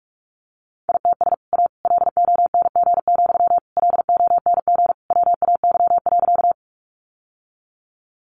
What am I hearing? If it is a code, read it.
Morse code, heard as a keyed tone, about 44 wpm, 700 Hz. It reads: ITS A LONG, LONG WAY.
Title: Sentences from Top 600 Words 45wpm